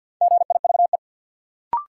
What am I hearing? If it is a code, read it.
Morse code, heard as a keyed tone, about 53 wpm, 700 Hz. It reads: GIVE
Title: Sets of 2 Words - New Word Drill - 50wpm